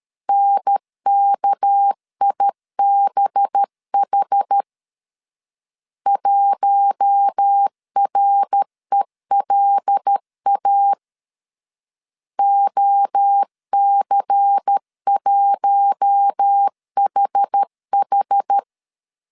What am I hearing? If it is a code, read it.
Morse code, heard as a keyed tone, about 13 wpm, 800 Hz. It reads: NKIBH 1RELA OC1HH
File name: CW / DET / 1 6 1500C S A 12wpm 3440